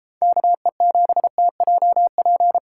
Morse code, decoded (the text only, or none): KE7TJP